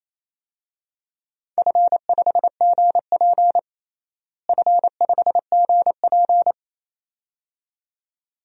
Morse code, heard as a keyed tone, about 28 wpm, 700 Hz. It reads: F5GP F5GP
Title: DXCC-100 Call Signs - 4 Characters - Repeated 2x - 28wpm